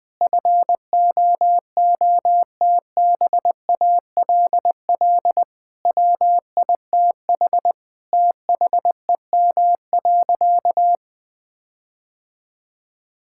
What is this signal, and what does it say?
Morse code, heard as a keyed tone, about 20 wpm, 700 Hz. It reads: FOOTBALL WITH THEM.
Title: Sentences from Top 1600 Words 20wpm